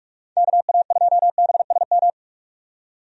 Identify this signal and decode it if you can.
Morse code, heard as a keyed tone, about 41 wpm, 700 Hz. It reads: KA2BSM
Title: USA Call Signs - 6 Characters - 45wpm